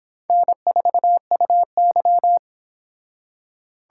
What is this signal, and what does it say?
Morse code, heard as a keyed tone, about 26 wpm, 700 Hz. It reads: N4UY